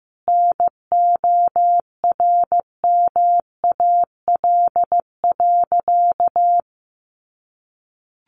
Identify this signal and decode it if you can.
Morse code, heard as a keyed tone, about 15 wpm, 700 Hz. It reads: NORMAL.
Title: Sentences from Top 1200 Words 15wpm